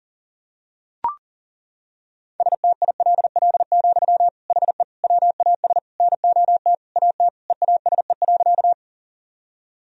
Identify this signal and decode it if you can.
Morse code, heard as a keyed tone, about 40 wpm, 700 Hz. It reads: STILL, HE WAS NOT AT EASE.
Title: Sentences from Top 1000 Words - Review Part 1 - 40wpm